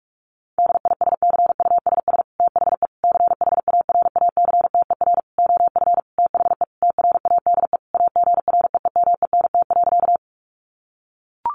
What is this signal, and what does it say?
Morse code, heard as a keyed tone, about 44 wpm, 700 Hz. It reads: DISCUSS THE CHARACTER OF THE TRADE AGREEMENT.
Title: Sentences from Top 1500 Words 45wpm